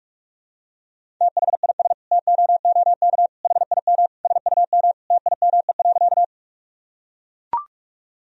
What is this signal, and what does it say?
Morse code, heard as a keyed tone, about 41 wpm, 700 Hz. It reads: THIS TOOK HIM SUM TIME.
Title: Sentences from Top 300 Words and Abbreviations 45wpm